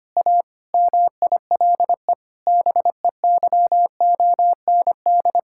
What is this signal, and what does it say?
Morse code, heard as a keyed tone, about 25 wpm, 700 Hz. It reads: A MILE BEYOND
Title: Sentences from Top 1000 Words - Review Part 6 - 25wpm